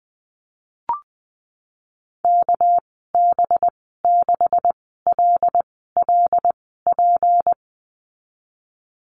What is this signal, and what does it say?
Morse code, heard as a keyed tone, about 20 wpm, spaced as for 15 wpm, 700 Hz. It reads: KB6LLP
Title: 200 Lesson 32g - Letter k - Call Signs 20wpm